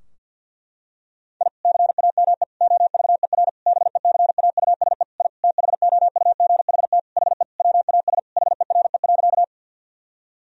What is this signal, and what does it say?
Morse code, heard as a keyed tone, about 50 wpm, 700 Hz. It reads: I CAME OVER BECAUSE I THOUGHT HE WAS HERE.
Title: Sentences from Top 1000 Words - Review Part 3 - 50wpm